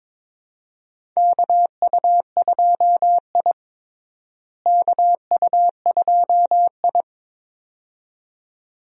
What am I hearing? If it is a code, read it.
Morse code, heard as a keyed tone, about 22 wpm, 700 Hz. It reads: KU2I KU2I
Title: USA Call Signs - 4 Characters - Repeated 2x - 22wpm